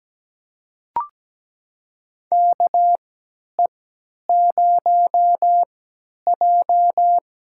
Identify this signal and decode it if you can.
Morse code, heard as a keyed tone, about 17 wpm, spaced as for 10 wpm, 700 Hz. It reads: KE0J
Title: USA Call Signs - 3x Character Spacing - 17wpm